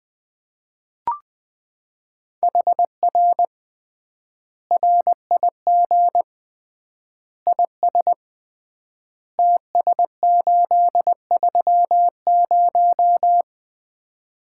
Morse code, heard as a keyed tone, about 20 wpm, 700 Hz. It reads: HR RIG IS TS830
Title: Common QSO Phrases - 3x Word-Spacing - 20wpm